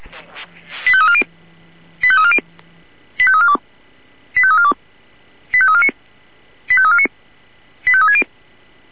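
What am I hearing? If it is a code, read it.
Five-tone selective call (ZVEI), heard as quick runs of five tones, tones of about 70 ms. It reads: 85340 85340 85342 85342 85349 85349 85360
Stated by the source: Received by SDR, ZVEI five-tone call 85330.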